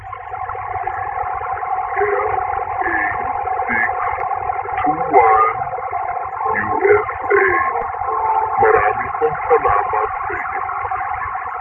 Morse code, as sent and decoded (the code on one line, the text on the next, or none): none